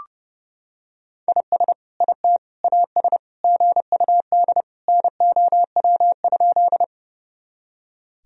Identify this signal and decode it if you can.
Morse code, heard as a keyed tone, about 30 wpm, 700 Hz. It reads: IS IT AS GUD NOW?
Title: Sentences from Top 200 Words and Abbreviations 30wpm